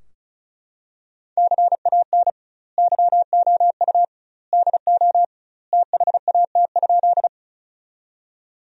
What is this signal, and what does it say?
Morse code, heard as a keyed tone, about 36 wpm, 700 Hz. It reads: CAN YOU DO THAT?
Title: 2x Word-Spacing - Sentences from Top 100 Words - 35wpm